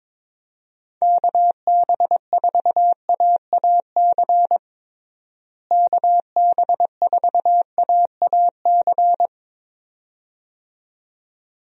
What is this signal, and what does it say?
Morse code, heard as a keyed tone, about 22 wpm, 700 Hz. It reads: KB4AAC KB4AAC
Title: USA Call Signs - 6 Characters - Repeated 2x - 22wpm